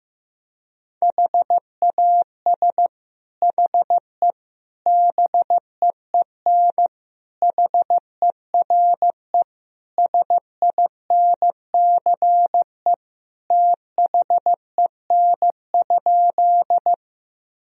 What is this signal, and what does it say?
Morse code, heard as a keyed tone, about 15 wpm, 700 Hz. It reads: HAS HE BEEN HERE SINCE THEN?
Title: Sentences from Top 1000 Words - Review Part 1 - 15wpm